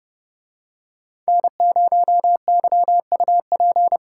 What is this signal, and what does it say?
Morse code, heard as a keyed tone, about 30 wpm, 700 Hz. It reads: N0YUP